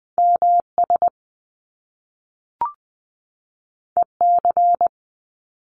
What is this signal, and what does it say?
Morse code, heard as a keyed tone, about 20 wpm, 700 Hz. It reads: M S E C
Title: Top Two Letter Combinations in Words 20wpm